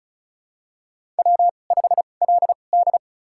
Morse code, heard as a keyed tone, about 36 wpm, spaced as for 25 wpm, 700 Hz. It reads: W5LD
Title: USA Call Signs - 2x Character Spacing - 35wpm